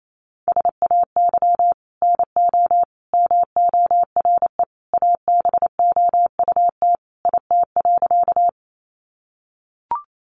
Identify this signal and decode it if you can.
Morse code, heard as a keyed tone, about 28 wpm, 700 Hz. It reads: SAY NO MORE ABOUT IT.
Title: Sentences from Top 200 Words 28wpm